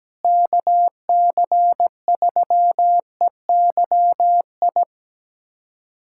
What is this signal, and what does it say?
Morse code, heard as a keyed tone, about 17 wpm, 700 Hz. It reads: KC3EYI
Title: USA Callsign DE Callsign - 6 Characters - 17wpm